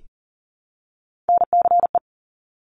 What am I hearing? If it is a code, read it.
Morse code, heard as a keyed tone, about 40 wpm, 700 Hz. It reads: NCE
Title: Top Three Letter Combinations in Words 40wpm